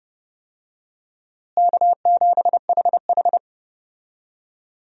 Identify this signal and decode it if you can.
Morse code, heard as a keyed tone, about 30 wpm, 700 Hz. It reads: K7HH
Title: USA Call Signs - 4 Characters - Repeated 2x - 30wpm